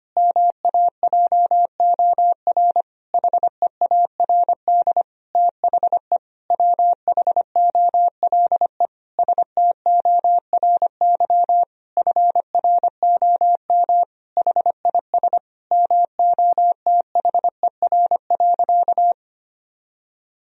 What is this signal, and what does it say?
Morse code, heard as a keyed tone, about 25 wpm, 700 Hz. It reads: MAJOR HEARD THE WHOLE STORY FROM HIS MOTHER.
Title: Sentences from Top 1000 Words - Review Part 4 - 25wpm